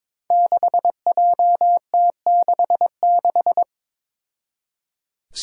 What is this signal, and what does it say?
Morse code, heard as a keyed tone, about 23 wpm, 700 Hz. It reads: 6JT66
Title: Five Challenging Letter-Number Combinations 22wpm